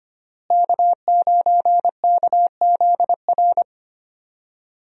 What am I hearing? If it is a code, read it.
Morse code, heard as a keyed tone, about 25 wpm, 700 Hz. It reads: K9KZR
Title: USA Call Signs - 5 Characters - Repeated 2x - 25wpm